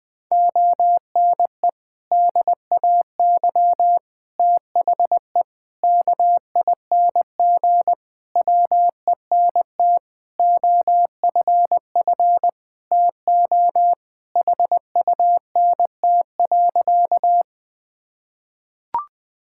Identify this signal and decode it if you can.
Morse code, heard as a keyed tone, about 21 wpm, 700 Hz. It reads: ONE DAY THE KING WENT OFF TO HUNT.
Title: Sentences from Top 1600 Words 20wpm